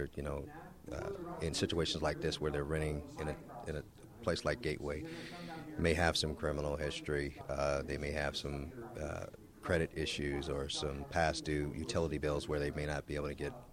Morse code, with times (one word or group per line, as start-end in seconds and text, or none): none